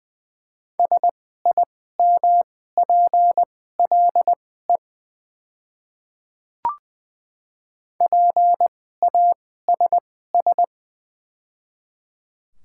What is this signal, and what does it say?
Morse code, heard as a keyed tone, about 20 wpm, spaced as for 15 wpm, 700 Hz. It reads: SIMPLE PASS
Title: 2x Character Spacing - Top 500 Words - 20wpm